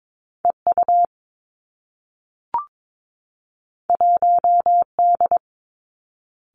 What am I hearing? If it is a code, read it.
Morse code, heard as a keyed tone, about 22 wpm, 700 Hz. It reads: E U 1 D
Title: Two Challenging Letter-Number Combinations 22wpm